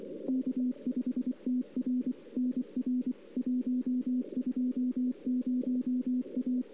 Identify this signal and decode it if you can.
Morse code, heard as a keyed tone, about 24 wpm, 250 Hz. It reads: K5TR NR 120 A
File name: CW / 2008-ss-cw-k5tr-op-n5rz -02:00